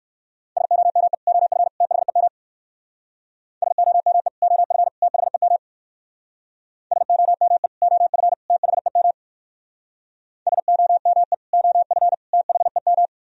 Morse code, heard as a keyed tone, about 57 wpm, 700 Hz. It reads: SOME OF THEM SOME OF THEM SOME OF THEM SOME OF THEM